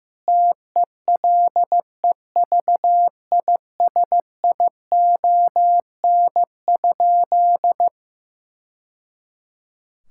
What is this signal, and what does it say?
Morse code, heard as a keyed tone, about 15 wpm, 700 Hz. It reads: TELEVISION?
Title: Sentences from Top 1000 Words - Review Part 5 - 15wpm